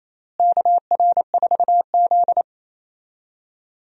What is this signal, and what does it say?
Morse code, heard as a keyed tone, about 28 wpm, 700 Hz. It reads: KR4Z
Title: USA Call Signs - 4 Characters - Repeated 3x - 28wpm